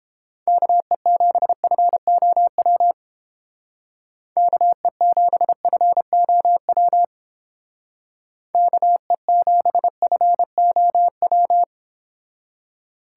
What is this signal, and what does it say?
Morse code, heard as a keyed tone, about 28 wpm, 700 Hz. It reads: KE7FOW KE7FOW KE7FOW